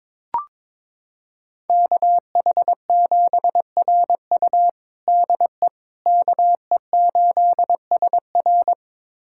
Morse code, heard as a keyed tone, about 21 wpm, 700 Hz. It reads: KH7RU DE KE8SR
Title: USA Callsign DE Callsign - 5 Characters - 22wpm